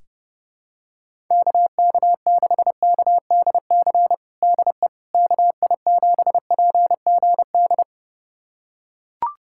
Morse code, heard as a keyed tone, about 32 wpm, 700 Hz. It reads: KK6KDC DE KI7PGD
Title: USA Callsign DE Callsign 30wpm